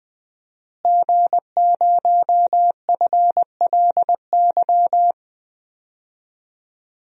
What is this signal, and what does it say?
Morse code, heard as a keyed tone, about 20 wpm, 700 Hz. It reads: G0FLY
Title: DXCC-100 Call Signs - 5 Characters - Repeated 3x - 20wpm